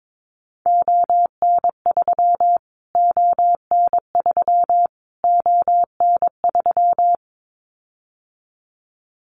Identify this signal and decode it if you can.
Morse code, heard as a keyed tone, about 22 wpm, 700 Hz. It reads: ON3 ON3 ON3
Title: Callsign Prefixes for Top 25 DXCC - Repeated 3x - 22wpm